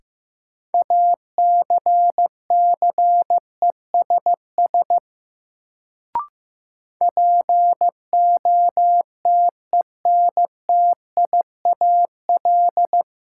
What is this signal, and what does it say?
Morse code, heard as a keyed tone, about 15 wpm, 700 Hz. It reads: ACCESS POTENTIAL
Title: Sentences from Top 1000 Words - Review Part 4 - New Word Drill - 15wpm